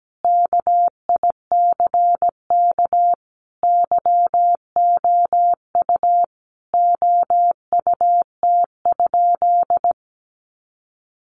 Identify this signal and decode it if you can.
Morse code, heard as a keyed tone, about 17 wpm, 700 Hz. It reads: KICK YOU OUT?
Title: Sentences from Top 1300 Words 17wpm